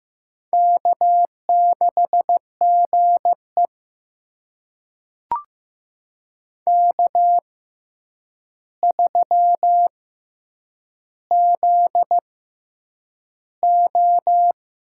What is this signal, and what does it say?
Morse code, heard as a keyed tone, about 15 wpm, spaced as for 8 wpm, 700 Hz. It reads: K6GE K3ZO
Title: USA Call Signs - 4 Characters - 6x Character Spacing - 15wpm